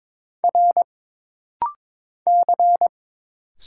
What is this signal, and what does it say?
Morse code, heard as a keyed tone, about 22 wpm, 700 Hz. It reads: R C
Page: This rapid-fire timing is geared to push your ICR (Instant Character Recognition) skills.